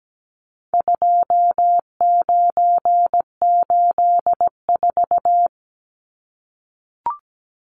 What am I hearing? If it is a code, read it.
Morse code, heard as a keyed tone, about 17 wpm, 700 Hz. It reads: 2984
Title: Numbers - 4-Digit 17wpm